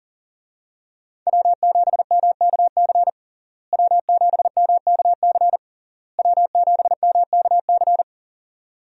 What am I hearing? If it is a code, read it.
Morse code, heard as a keyed tone, about 37 wpm, 700 Hz. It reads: W7MKC W7MKC W7MKC
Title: USA Call Signs - 5 Characters - Repeated 3x - 40wpm